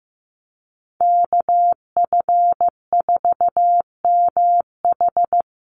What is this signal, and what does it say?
Morse code, heard as a keyed tone, about 15 wpm, 700 Hz. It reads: KF4MH
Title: NA SOTA Chaser Callsigns 15wpm